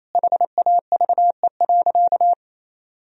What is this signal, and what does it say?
Morse code, heard as a keyed tone, about 28 wpm, 700 Hz. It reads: HAVE.
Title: Sentences from Top 1000 Words - Review Part 2 - 28wpm